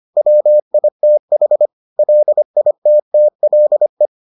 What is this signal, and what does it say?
Morse code, heard as a keyed tone, about 25 wpm, 600 Hz. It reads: WITH LITTLE